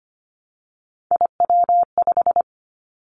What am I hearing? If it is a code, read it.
Morse code, heard as a keyed tone, about 25 wpm, 700 Hz. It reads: IW5
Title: Three Challenging Letter-Number Combinations 25wpm